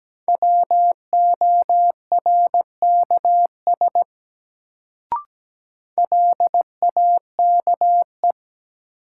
Morse code, heard as a keyed tone, about 17 wpm, 700 Hz. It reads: WORKS LAKE
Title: Sentences from Top 1100 Words - New Word Drill Alt - 17wpm